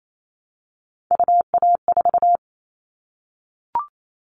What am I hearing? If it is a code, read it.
Morse code, heard as a keyed tone, about 28 wpm, 700 Hz. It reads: UA4
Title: Callsign Prefixes for Top 100 DXCC - 28wpm